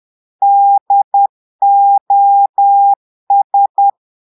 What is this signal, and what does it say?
Morse code, heard as a keyed tone, about 10 wpm, 800 Hz. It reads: DOS
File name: Portugais - texte en clair 1 - vitesse - 10 mots minute